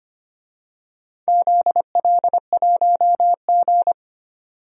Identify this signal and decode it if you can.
Morse code, heard as a keyed tone, about 25 wpm, 700 Hz. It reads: ZL1G
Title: DXCC-100 Call Signs - 4 Characters - 25wpm